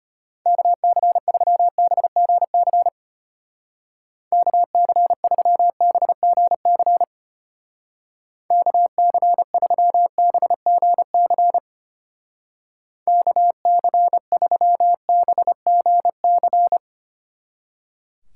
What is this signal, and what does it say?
Morse code, heard as a keyed tone, about 31 wpm, 700 Hz. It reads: KC3BGC KC3BGC KC3BGC KC3BGC
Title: Speed Racing - USA Call Signs 25wpm